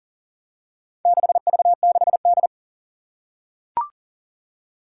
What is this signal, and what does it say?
Morse code, heard as a keyed tone, about 40 wpm, 700 Hz. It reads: 6V6D
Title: Four Challenging Letter-Number Combinations 40wpm